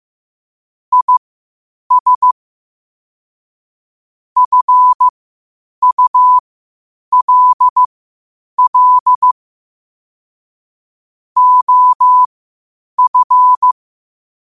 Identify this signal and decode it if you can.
Morse code, heard as a keyed tone, about 15 wpm, spaced as for 8 wpm, 1000 Hz. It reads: IS FULL OF
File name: qotdfiles / 8wpm